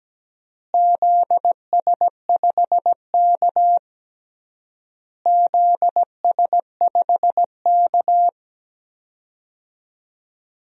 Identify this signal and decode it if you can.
Morse code, heard as a keyed tone, about 17 wpm, 700 Hz. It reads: ZS5K ZS5K
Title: DXCC-100 Call Signs - 4 Characters - Repeated 3x - 17wpm